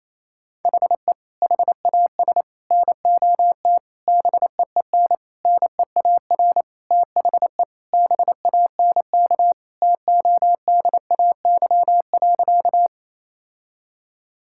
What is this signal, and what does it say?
Morse code, heard as a keyed tone, about 27 wpm, 700 Hz. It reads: HE HAS NOT BEEN NEAR THE BANK TODAY.
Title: Sentences from Top 1000 Words - Review Part 6 - 28wpm